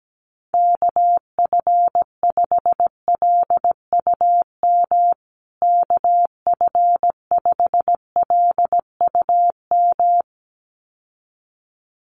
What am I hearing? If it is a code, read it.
Morse code, heard as a keyed tone, about 17 wpm, 700 Hz. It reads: KF5LUM KF5LUM
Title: USA Call Signs Repeated 2x 17wpm